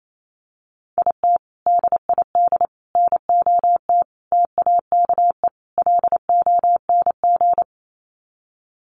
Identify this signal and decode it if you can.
Morse code, heard as a keyed tone, about 28 wpm, 700 Hz. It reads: IT DID NOT TAKE LONG
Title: Sets of 5 Words Encore2 28wpm